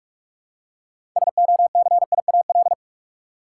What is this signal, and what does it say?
Morse code, heard as a keyed tone, about 46 wpm, 700 Hz. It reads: SOCIAL
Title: K1USN SST QSO Practice 45wpm